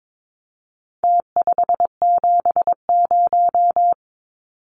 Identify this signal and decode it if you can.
Morse code, heard as a keyed tone, about 22 wpm, 700 Hz. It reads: T570
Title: Numbers - Contest Style - 4-Digit 22wpm